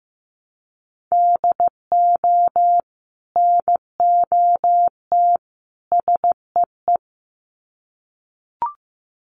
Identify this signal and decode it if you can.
Morse code, heard as a keyed tone, about 15 wpm, 700 Hz. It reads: DO NOT SEE
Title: Sets of 3 Words 15wpm